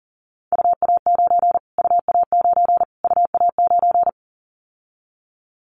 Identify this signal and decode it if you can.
Morse code, heard as a keyed tone, about 40 wpm, 700 Hz. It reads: UA9 UA9 UA9
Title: Callsign Prefixes for Top 25 DXCC - Repeated 3x - 40wpm